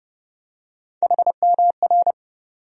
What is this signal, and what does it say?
Morse code, heard as a keyed tone, about 30 wpm, 700 Hz. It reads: HMR
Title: USA Call Sign Postfixes - 3 Letters 30wpm